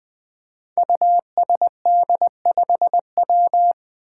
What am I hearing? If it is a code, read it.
Morse code, heard as a keyed tone, about 20 wpm, 700 Hz. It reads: USD5W
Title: Five Challenging Letter-Number Combinations 20wpm